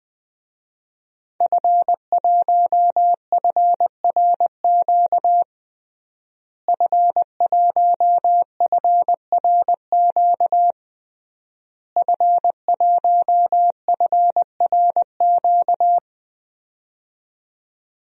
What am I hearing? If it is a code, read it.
Morse code, heard as a keyed tone, about 21 wpm, 700 Hz. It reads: F1FRQ F1FRQ F1FRQ
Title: DXCC-100 Call Signs - 5 Characters - Repeated 3x - 20wpm